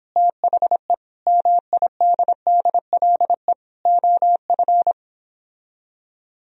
Morse code, heard as a keyed tone, about 29 wpm, 700 Hz. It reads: THE MIDDLE OF